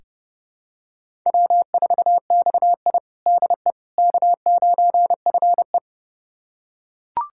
Morse code, heard as a keyed tone, about 30 wpm, 700 Hz. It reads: W4XI DE K9FE
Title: USA Callsign DE Callsign - 4 Characters - 30wpm